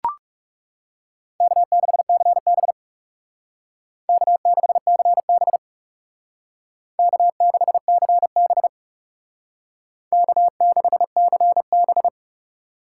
Morse code, heard as a keyed tone, about 36 wpm, 700 Hz. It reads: K6CB K6CB K6CB K6CB
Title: Speed Racing - USA Call Signs 30wpm